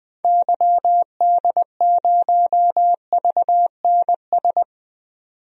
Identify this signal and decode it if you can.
Morse code, heard as a keyed tone, about 20 wpm, 700 Hz. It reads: YD0VNS
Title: DXCC-100 Call Signs - 6 Characters - 20wpm